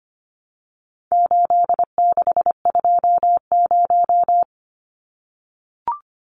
Morse code, heard as a keyed tone, about 25 wpm, 700 Hz. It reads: 8620
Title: Numbers - 4-Digit 25wpm